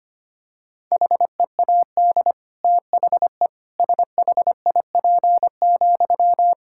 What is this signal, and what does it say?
Morse code, heard as a keyed tone, about 25 wpm, 700 Hz. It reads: HEAD THE SHIP,
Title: Sentences from Top 1000 Words - Review Part 5 - 25wpm